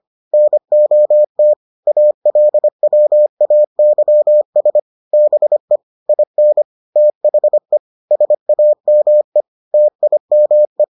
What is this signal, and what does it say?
Morse code, heard as a keyed tone, about 25 wpm, 600 Hz. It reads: NOT ALWAYS BE IN THE SAME TIME